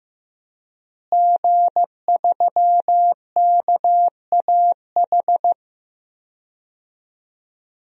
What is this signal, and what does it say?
Morse code, heard as a keyed tone, about 15 wpm, 700 Hz. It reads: G3KAH
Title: DXCC-100 Call Signs - 5 Characters - Repeated 2x - 15wpm